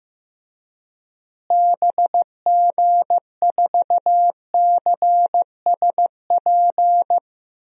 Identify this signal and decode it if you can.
Morse code, heard as a keyed tone, about 15 wpm, 700 Hz. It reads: BG4CSP
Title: DXCC-100 Call Signs - 6 Characters - Repeated 3x - 15wpm